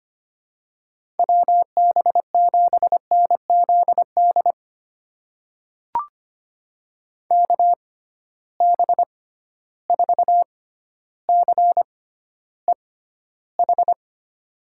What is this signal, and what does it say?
Morse code, heard as a keyed tone, about 25 wpm, spaced as for 13 wpm, 700 Hz. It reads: WB7NZD KB4CEH
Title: USA Call Signs - 6 Characters - 6x Character Spacing - 25wpm